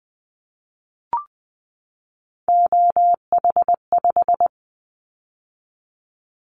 Morse code, heard as a keyed tone, about 20 wpm, 700 Hz. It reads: OH5
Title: Callsign Prefixes for Top 50 DXCC - 20wpm